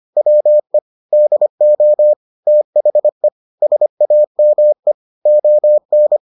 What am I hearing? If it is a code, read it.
Morse code, heard as a keyed tone, about 25 wpm, 600 Hz. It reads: WE DO THE SAME ON